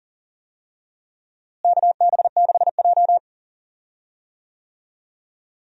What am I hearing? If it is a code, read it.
Morse code, heard as a keyed tone, about 40 wpm, 700 Hz. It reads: KB6J
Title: USA Call Signs - 4 Characters - Repeated 2x - 40wpm